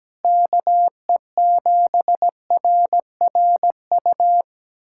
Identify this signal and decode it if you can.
Morse code, heard as a keyed tone, about 17 wpm, 700 Hz. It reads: KE7RRU
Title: USA Callsign DE Callsign 17wpm